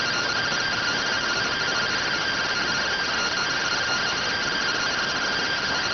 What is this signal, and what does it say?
Two-tone teleprinter signal, about 170 Hz shift.